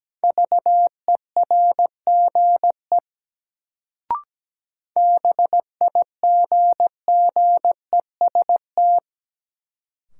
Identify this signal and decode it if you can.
Morse code, heard as a keyed tone, about 17 wpm, 700 Hz. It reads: VERGE BIGGEST
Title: Sets of 4 Words - New Word Drill - 17wpm